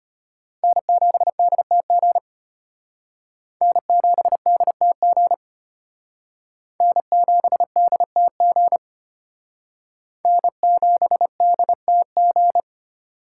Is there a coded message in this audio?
Yes: Morse code from 0.6 to 12.6 s, heard as a keyed tone, about 33 wpm, 700 Hz. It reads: N7DTG N7DTG N7DTG N7DTG